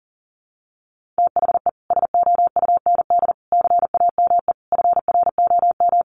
Morse code, heard as a keyed tone, about 40 wpm, 700 Hz. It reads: THE SOUND CAME FROM
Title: Sentences from Top 1400 Words 40wpm